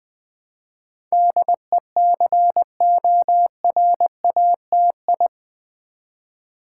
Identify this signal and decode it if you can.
Morse code, heard as a keyed tone, about 20 wpm, 700 Hz. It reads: DECORATI
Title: Build A Word - 4 Letters At A Time - 10-Letter Words - 20wpm